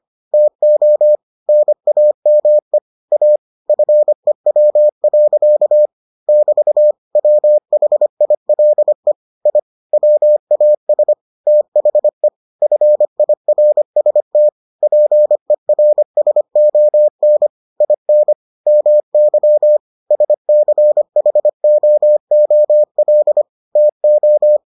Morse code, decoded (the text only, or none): TO NAME A FEW. <BT> WHILE I WAS THE FIRST PERSON IN MY SCHOOL TO